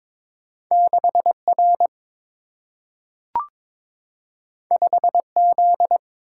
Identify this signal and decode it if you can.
Morse code, heard as a keyed tone, about 22 wpm, 700 Hz. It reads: 6R 5Z
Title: Single-Letter Single-Number 22wpm